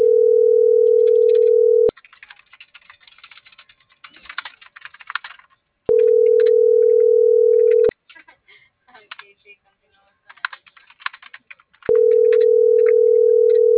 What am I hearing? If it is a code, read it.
Ringback tone (North America), 440 + 480 Hz.